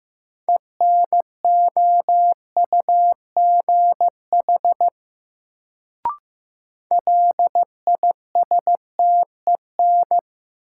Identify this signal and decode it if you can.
Morse code, heard as a keyed tone, about 15 wpm, 700 Hz. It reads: ENOUGH LISTEN
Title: Sentences from Top 400 Words - New Word Drill - 15wpm